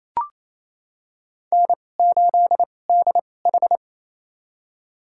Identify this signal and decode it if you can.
Morse code, heard as a keyed tone, about 27 wpm, spaced as for 20 wpm, 700 Hz. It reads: N8DH
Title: USA Call Signs - 4 Characters - 2x Character Spacing - 28wpm